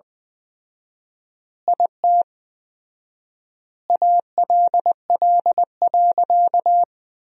Morse code, heard as a keyed tone, about 20 wpm, 700 Hz. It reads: IT ALL.
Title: 4x Word-Spacing - Sentences from Top 300 Words - 20wpm